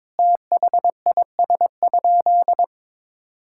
Morse code, heard as a keyed tone, about 22 wpm, 700 Hz. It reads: THIS?
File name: practice / 22wpm / Sentences with 2-Word Phrases - 22wpm